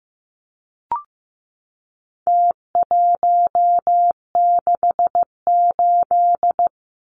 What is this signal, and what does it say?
Morse code, heard as a keyed tone, about 15 wpm, 700 Hz. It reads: T168
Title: Numbers - Contest Style - 4-Digit 15wpm